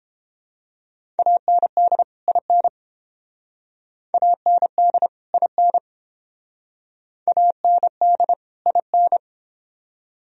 Morse code, heard as a keyed tone, about 31 wpm, 700 Hz. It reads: AND IN AND IN AND IN